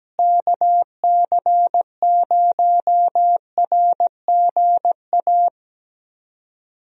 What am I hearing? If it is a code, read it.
Morse code, heard as a keyed tone, about 17 wpm, 700 Hz. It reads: KC0RGA
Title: USA Call Signs - 6 Characters - Repeated 2x - 17wpm